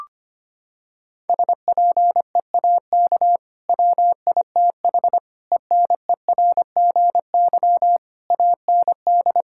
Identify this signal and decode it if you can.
Morse code, heard as a keyed tone, about 25 wpm, 700 Hz. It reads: SPEAK WITH ENERGY AND
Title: Sentences from Top 1000 Words - Review Part 6 - 25wpm